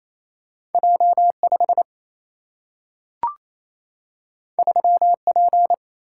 Morse code, heard as a keyed tone, about 28 wpm, 700 Hz. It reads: J 5 3 P
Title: Single-Letter Single-Number 28wpm